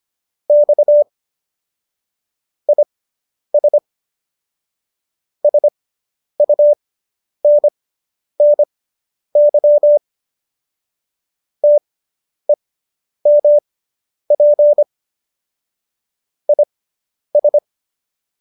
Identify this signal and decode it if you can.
Morse code, heard as a keyed tone, about 25 wpm, spaced as for 10 wpm, 590 Hz. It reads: X IS SUNNY TEMP IS